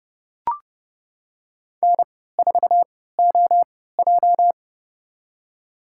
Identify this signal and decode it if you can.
Morse code, heard as a keyed tone, about 30 wpm, spaced as for 17 wpm, 700 Hz. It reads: N4OJ
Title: USA Call Signs - 3x Character Spacing - 30wpm